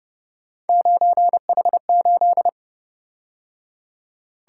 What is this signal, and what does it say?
Morse code, heard as a keyed tone, about 29 wpm, 700 Hz. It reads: 9H8
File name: Speed-Racing - Three Letter-Number Combinations - 25wpm